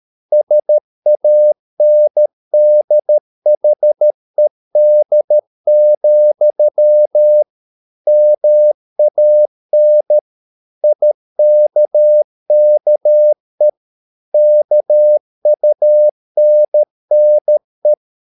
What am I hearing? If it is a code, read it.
Morse code, heard as a keyed tone, about 13 wpm, 600 Hz. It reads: SANDHED, MAN IKKE KUNNE
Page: Ferie 13 wpm